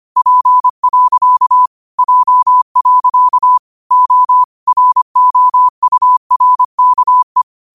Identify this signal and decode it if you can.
Morse code, heard as a keyed tone, about 25 wpm, 1000 Hz. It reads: P. J. O ROURKE
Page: Quotes for Fri, 15 Aug 2025 in Morse Code at 25 words per minute.